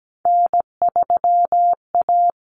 Morse code, heard as a keyed tone, about 17 wpm, 700 Hz. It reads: N3A
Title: Common Cut Numbers - 3-Digit 17wpm